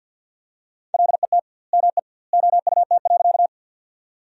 Morse code, heard as a keyed tone, about 50 wpm, 700 Hz. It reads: LET ME OUT.
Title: 2x Word-Spacing - Sentences from Top 300 Words - 50wpm